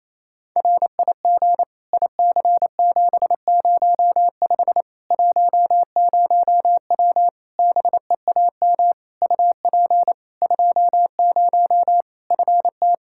Morse code, heard as a keyed tone, about 28 wpm, 700 Hz. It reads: RIG IC705 10W BEAM UP 20 FT
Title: Practice QSO - Exchange 2 - Info - 28wpm